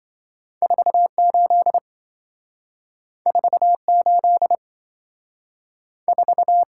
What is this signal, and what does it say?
Morse code, heard as a keyed tone, about 27 wpm, 700 Hz. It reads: 48 48 4